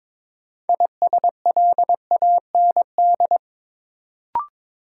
Morse code, heard as a keyed tone, about 22 wpm, 700 Hz. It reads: ISLAND
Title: Sentences from Top 500 Words - New Word Drill - 22wpm